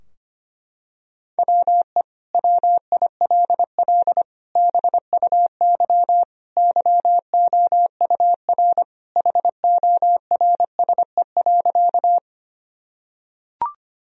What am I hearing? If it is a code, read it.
Morse code, heard as a keyed tone, about 25 wpm, 700 Hz. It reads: WE WILL BUY YOUR HORSE.
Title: Sentences from Top 1000 Words - Review Part 4 - 25wpm